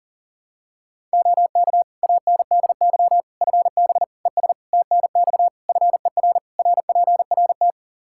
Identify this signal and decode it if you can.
Morse code, heard as a keyed tone, about 40 wpm, 700 Hz. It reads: OK ANDY FB ES TNX FER RPRT
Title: Practice QSO - Exchange 2 - Info - 40wpm